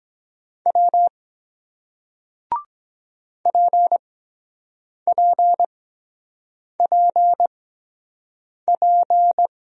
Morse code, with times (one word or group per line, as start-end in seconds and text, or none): 0.7-1.1 W
3.5-9.5 PPPP